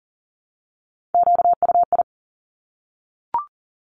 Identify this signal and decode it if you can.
Morse code, heard as a keyed tone, about 38 wpm, 700 Hz. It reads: QUI
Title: Top Three Letter Combinations in Words 40wpm